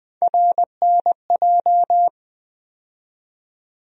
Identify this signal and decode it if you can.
Morse code, heard as a keyed tone, about 20 wpm, 700 Hz. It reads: RNJ
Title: USA Call Sign Postfixes - 3 Letters 20wpm